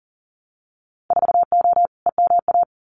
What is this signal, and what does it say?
Morse code, heard as a keyed tone, about 40 wpm, 700 Hz. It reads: 4O EMA
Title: Winter Field Day Exchanges 40wpm